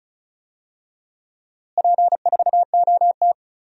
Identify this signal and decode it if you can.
Morse code, heard as a keyed tone, about 34 wpm, 700 Hz. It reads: P4OT
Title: DXCC-100 Call Signs - 4 Characters - Repeated 2x - 35wpm